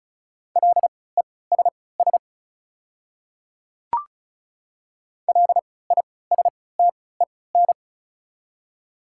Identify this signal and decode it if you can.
Morse code, heard as a keyed tone, about 36 wpm, spaced as for 20 wpm, 700 Hz. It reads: LESS LISTEN
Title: 3x Character Spacing - Top 500 Words - 35wpm